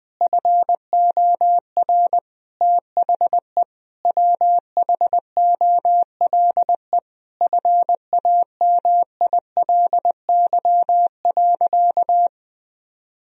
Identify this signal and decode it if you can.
Morse code, heard as a keyed tone, about 20 wpm, 700 Hz. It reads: FOR THE WHOLE FAMILY.
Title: Sentences from Top 1000 Words - Review Part 4 - 20wpm